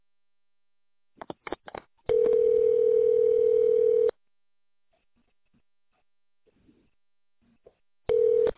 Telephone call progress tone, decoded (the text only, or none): ringback tone